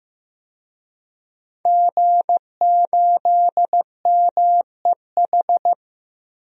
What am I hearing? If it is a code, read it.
Morse code, heard as a keyed tone, about 15 wpm, 700 Hz. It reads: G8MEH
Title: DXCC-100 Call Signs - 5 Characters - Repeated 3x - 15wpm